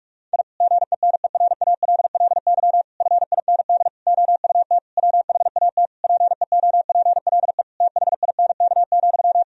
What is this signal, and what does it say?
Morse code, heard as a keyed tone, about 46 wpm, 700 Hz. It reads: I GENERALLY FIND OUT WHAT PEOPLE THINK,
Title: Sentences from Top 1000 Words - Review Part 4 - 45wpm